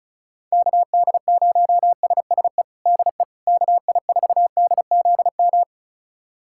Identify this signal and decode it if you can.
Morse code, heard as a keyed tone, about 36 wpm, 700 Hz. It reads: KD0SSE DE KI4DZM
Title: USA Callsign DE Callsign 35wpm